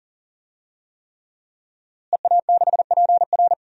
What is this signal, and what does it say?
Morse code, heard as a keyed tone, about 40 wpm, 700 Hz. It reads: EA6PR
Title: DXCC-100 Call Signs - 5 Characters - Repeated 3x - 40wpm